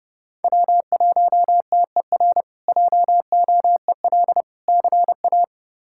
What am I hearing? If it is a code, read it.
Morse code, heard as a keyed tone, about 32 wpm, 700 Hz. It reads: W1TER JOEL CA
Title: K1USN SST QSO Practice - 30wpm